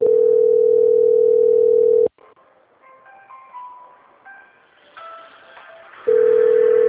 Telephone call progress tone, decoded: ringback tone

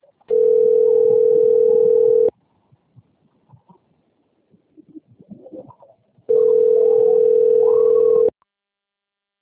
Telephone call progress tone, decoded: ringback tone

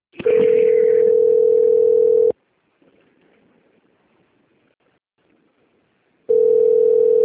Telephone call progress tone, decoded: ringback tone

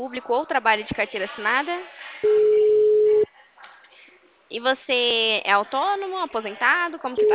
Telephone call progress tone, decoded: ringback tone

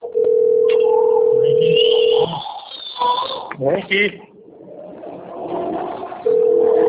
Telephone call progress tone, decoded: ringback tone

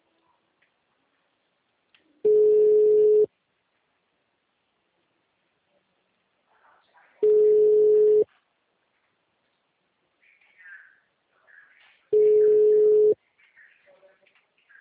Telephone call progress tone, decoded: ringback tone

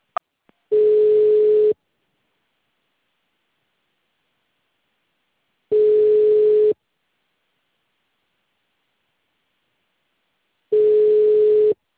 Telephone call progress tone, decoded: ringback tone

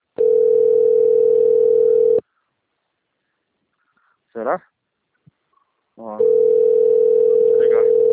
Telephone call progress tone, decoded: ringback tone

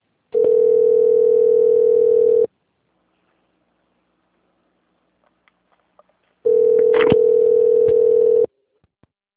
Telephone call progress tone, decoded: ringback tone

